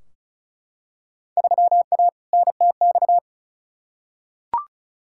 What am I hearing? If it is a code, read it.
Morse code, heard as a keyed tone, about 36 wpm, 700 Hz. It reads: 3A NTX
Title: Field Day Exchanges - 35wpm